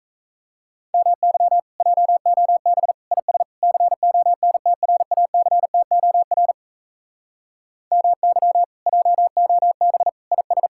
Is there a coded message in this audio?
Yes: Morse code from 0.9 to 10.7 s, heard as a keyed tone, about 38 wpm, 700 Hz. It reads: MY JOB IS CONTRACTOR MY JOB IS